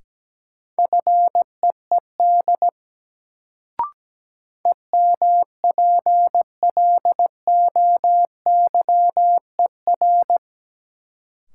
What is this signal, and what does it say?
Morse code, heard as a keyed tone, about 17 wpm, 700 Hz. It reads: FEED EMPLOYER
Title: Sentences from Top 1600 Words - New Word Drill - 17wpm